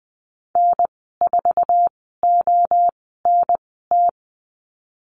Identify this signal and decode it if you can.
Morse code, heard as a keyed tone, about 20 wpm, spaced as for 15 wpm, 700 Hz. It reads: N4ONT
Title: 055 Lesson 10g - Letter r - Call Signs 20wpm